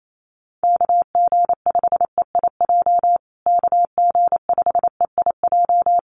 Morse code, heard as a keyed tone, about 28 wpm, 700 Hz. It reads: KG5EIJ KG5EIJ
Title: USA Call Signs Repeated 3x 28wpm